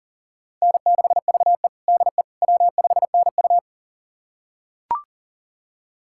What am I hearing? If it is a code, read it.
Morse code, heard as a keyed tone, about 37 wpm, 700 Hz. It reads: N6VE DE W5NU
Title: USA Callsign DE Callsign - 4 Characters - 40wpm